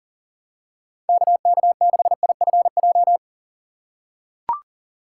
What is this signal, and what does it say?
Morse code, heard as a keyed tone, about 40 wpm, 700 Hz. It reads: KK6IFJ
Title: USA Call Signs - 6x Character Spacing - 40wpm